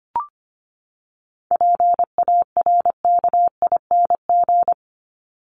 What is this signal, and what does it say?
Morse code, heard as a keyed tone, about 25 wpm, 700 Hz. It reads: PARKING
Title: Top 1501-1750 Words 25wpm